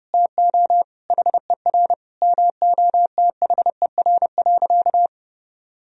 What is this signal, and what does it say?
Morse code, heard as a keyed tone, about 28 wpm, 700 Hz. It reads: TO HER MOTHER.
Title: Sentences from Top 1000 Words - Review Part 2 - 30wpm